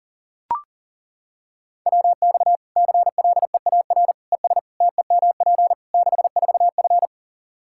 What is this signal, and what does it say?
Morse code, heard as a keyed tone, about 40 wpm, 700 Hz. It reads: WX CLEAR ES TEMP 64F
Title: K1USN SST QSO Practice 40wpm